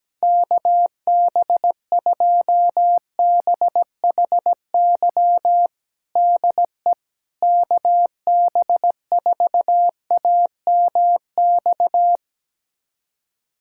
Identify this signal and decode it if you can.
Morse code, heard as a keyed tone, about 17 wpm, 700 Hz. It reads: KB2BHY DE KB4AMX
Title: USA Callsign DE Callsign - 6 Characters - 17wpm